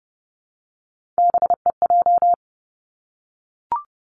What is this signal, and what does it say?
Morse code, heard as a keyed tone, about 30 wpm, 700 Hz. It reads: BEJ
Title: Three Challenging Letter-Number Combinations 30wpm